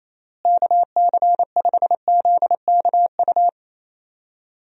Morse code, heard as a keyed tone, about 27 wpm, 700 Hz. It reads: KC5ZKU
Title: USA Call Signs - 6 Characters - Repeated 3x - 28wpm